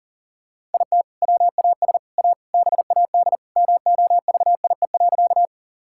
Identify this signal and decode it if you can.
Morse code, heard as a keyed tone, about 40 wpm, 700 Hz. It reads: IT WAS A BAD MOVIE.
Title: Sentences from Top 1000 Words - Review Part 5 - 40wpm